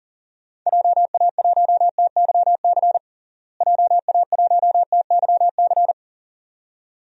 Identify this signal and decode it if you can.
Morse code, heard as a keyed tone, about 40 wpm, 700 Hz. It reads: JA1TYC JA1TYC
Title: DXCC-100 Call Signs - 6 Characters - Repeated 2x - 40wpm